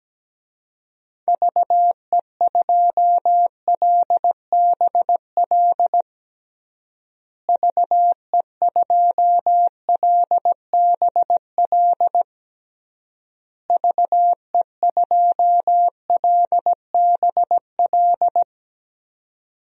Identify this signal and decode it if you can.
Morse code, heard as a keyed tone, about 17 wpm, 700 Hz. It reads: VE2LBL VE2LBL VE2LBL
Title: DXCC-100 Call Signs - 6 Characters - Repeated 3x - 17wpm